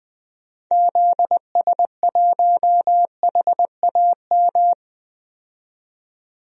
Morse code, heard as a keyed tone, about 20 wpm, 700 Hz. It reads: ZS1HAM
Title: DXCC-100 Call Signs - 6 Characters - 20wpm